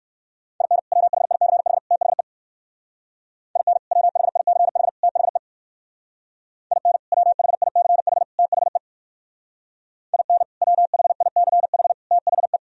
Morse code, heard as a keyed tone, about 65 wpm, 700 Hz. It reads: IN WHICH THE IN WHICH THE IN WHICH THE IN WHICH THE